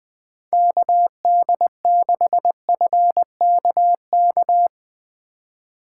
Morse code, heard as a keyed tone, about 20 wpm, 700 Hz. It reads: KD6FKK
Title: Speed Racing - USA 2x3 Call Signs 17wpm